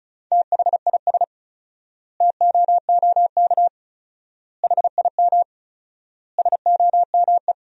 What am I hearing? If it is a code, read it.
Morse code, heard as a keyed tone, about 34 wpm, 700 Hz. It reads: THIS TOOK HIM SOME
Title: 4x Word-Spacing - Sentences from Top 300 Words - 35wpm